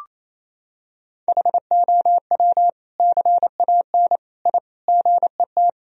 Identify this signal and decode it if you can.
Morse code, heard as a keyed tone, about 28 wpm, 700 Hz. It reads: HOW CAN I GET
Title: Sentences from Top 1000 Words - Review Part 4 - 28wpm